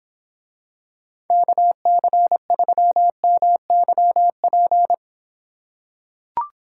Morse code, heard as a keyed tone, about 26 wpm, 700 Hz. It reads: KC3MYP